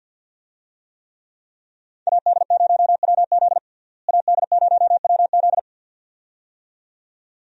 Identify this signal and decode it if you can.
Morse code, heard as a keyed tone, about 46 wpm, 700 Hz. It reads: AD0WZ AD0WZ
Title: USA Call Signs - 5 Characters - Repeated 2x - 50wpm